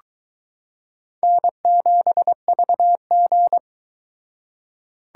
Morse code, heard as a keyed tone, about 22 wpm, 700 Hz. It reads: N7VG